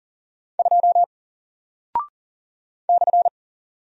Morse code, heard as a keyed tone, about 40 wpm, 700 Hz. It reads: 2 /
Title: Single Character (Letter:Number:Punctuation:Prosign) Alt 40wpm